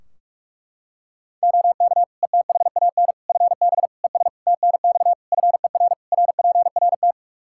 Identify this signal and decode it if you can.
Morse code, heard as a keyed tone, about 46 wpm, 700 Hz. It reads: OK ETHAN FB ES TNX FER RPRT
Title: Practice QSO - Exchange 2 - Info - 45wpm